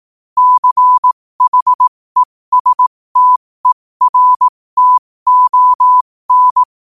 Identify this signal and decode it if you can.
Morse code, heard as a keyed tone, about 18 wpm, spaced as for 15 wpm, 1000 Hz. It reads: CHESTERTON
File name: qotdfiles / 15wpm